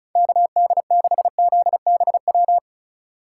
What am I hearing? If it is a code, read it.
Morse code, heard as a keyed tone, about 36 wpm, 700 Hz. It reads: KD6ZBW
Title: USA Call Signs - 6 Characters - 4x Character Spacing - 35wpm